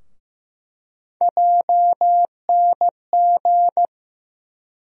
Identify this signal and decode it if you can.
Morse code, heard as a keyed tone, about 15 wpm, 700 Hz. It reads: JNG
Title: USA Call Sign Postfixes - 3 Letters 15wpm